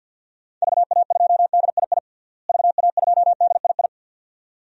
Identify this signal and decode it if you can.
Morse code, heard as a keyed tone, about 46 wpm, 700 Hz. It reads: VA2DII VA2DII
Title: DXCC-100 Call Signs - 6 Characters - Repeated 2x - 50wpm